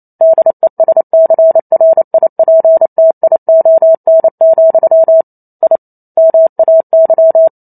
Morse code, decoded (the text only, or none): DESCRIPTION, I MAY